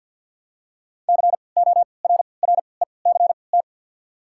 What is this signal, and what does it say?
Morse code, heard as a keyed tone, about 53 wpm, spaced as for 29 wpm, 700 Hz. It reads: CORRECT
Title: 3x Character Spacing - Top 500 Words - 50wpm